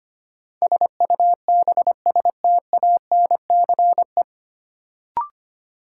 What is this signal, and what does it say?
Morse code, heard as a keyed tone, about 25 wpm, 700 Hz. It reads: SUBSTANCE
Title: Sentences from Top 1500 Words - New Word Drill Alt - 25wpm